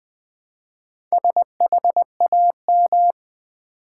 Morse code, heard as a keyed tone, about 20 wpm, 700 Hz. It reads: SHAM
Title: Build A Word - 1 Letter At A Time - 5-Letter Words - 20wpm